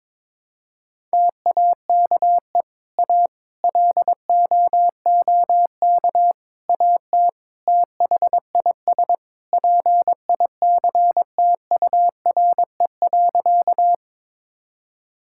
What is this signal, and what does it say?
Morse code, heard as a keyed tone, about 22 wpm, 700 Hz. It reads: TAKE A LOOK AT THIS PICTURE.
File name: practice / 22wpm / Sentences with 4-Word Phrases - 22wpm